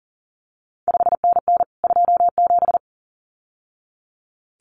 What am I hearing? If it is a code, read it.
Morse code, heard as a keyed tone, about 41 wpm, 700 Hz. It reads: 5NN 27
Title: CQ WW DX Contest Exchange Encore - 40wpm